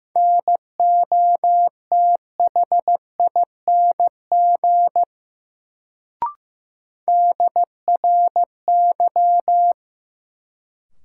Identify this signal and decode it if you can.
Morse code, heard as a keyed tone, about 15 wpm, 700 Hz. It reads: NOTHING DRY
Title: Sentences from Top 500 Words - New Word Drill - 15wpm